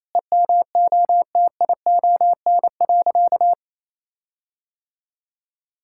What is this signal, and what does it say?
Morse code, heard as a keyed tone, about 28 wpm, 700 Hz. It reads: EMOTION.
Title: Sentences from Top 1600 Words 28wpm